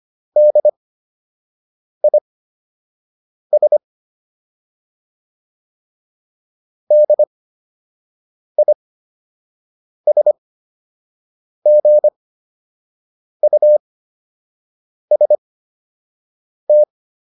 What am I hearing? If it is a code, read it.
Morse code, heard as a keyed tone, about 25 wpm, spaced as for 6 wpm, 600 Hz. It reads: DIS DISGUST